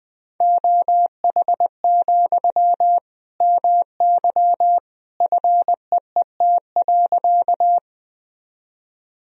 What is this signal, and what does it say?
Morse code, heard as a keyed tone, about 20 wpm, 700 Hz. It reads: OH, MY FEET.
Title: Sentences from Top 1000 Words - Review Part 1 - 20wpm